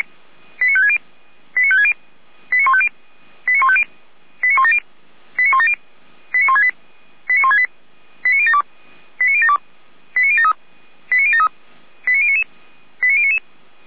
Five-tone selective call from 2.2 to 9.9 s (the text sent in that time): 89150 89150 89170 89170 89168 89168 89072 89072